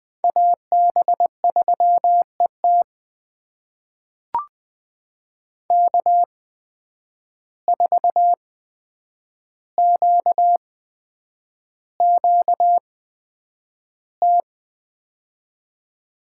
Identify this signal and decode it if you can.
Morse code, heard as a keyed tone, about 21 wpm, spaced as for 9 wpm, 700 Hz. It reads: AB3ET K4QQT
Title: USA Call Signs - 5 Characters - 8x Character Spacing - 20wpm